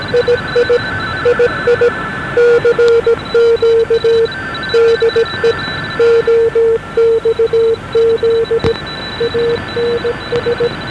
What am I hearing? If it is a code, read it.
Morse code, heard as a keyed tone, about 17 wpm, 460 Hz, with noise. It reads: II II CQ DE OXZ ANS
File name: OXZ41 8598 1141